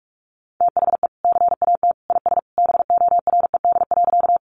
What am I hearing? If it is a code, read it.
Morse code, heard as a keyed tone, about 44 wpm, 700 Hz. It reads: THE CAT IS BORED.
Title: Easy Phrases 45wpm